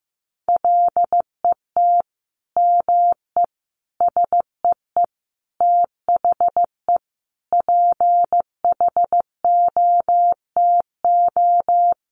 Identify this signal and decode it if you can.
Morse code, heard as a keyed tone, about 15 wpm, 700 Hz. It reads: LET ME SEE THE PHOTO
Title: Sentences from Top 1200 Words 15wpm